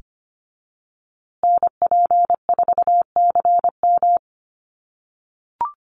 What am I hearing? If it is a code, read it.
Morse code, heard as a keyed tone, about 25 wpm, 700 Hz. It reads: NP4CM
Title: USA Call Signs Repeated 3x 25wpm